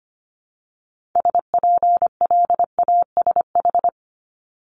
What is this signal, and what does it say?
Morse code, heard as a keyed tone, about 25 wpm, 700 Hz. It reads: SPLASH
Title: Multi-letter Phonemes 25wpm